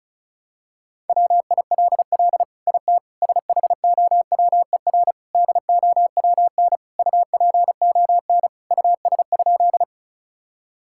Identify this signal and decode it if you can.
Morse code, heard as a keyed tone, about 34 wpm, 700 Hz. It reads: WILL IT SHOWER DOWN UPON US?
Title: Sentences from Top 1600 Words 35wpm